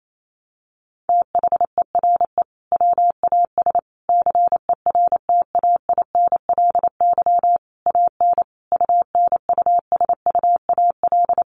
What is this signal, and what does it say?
Morse code, heard as a keyed tone, about 28 wpm, 700 Hz. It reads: THERE WAS CERTAINLY AN UNUSUAL
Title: Sentences from Top 1400 Words 28wpm